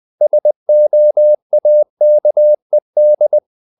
Morse code, heard as a keyed tone, about 20 wpm, 600 Hz. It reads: SOAKED
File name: the old man and the sea part 2 20wpm